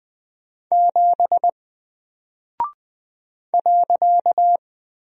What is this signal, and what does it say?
Morse code, heard as a keyed tone, about 20 wpm, 700 Hz. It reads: 7 .
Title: Single Character (Letter:Number:Punctuation:Prosign) 20wpm